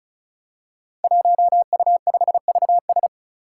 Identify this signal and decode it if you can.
Morse code, heard as a keyed tone, about 36 wpm, 700 Hz. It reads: 1U5VS
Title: Five Challenging Letter-Number Combinations 35wpm